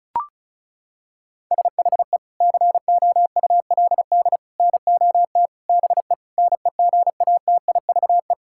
Morse code, heard as a keyed tone, about 36 wpm, 700 Hz. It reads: SHE COULD NOT BE NEGATIVE
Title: Sentences from Top 1000 Words - Review Part 6 - 35wpm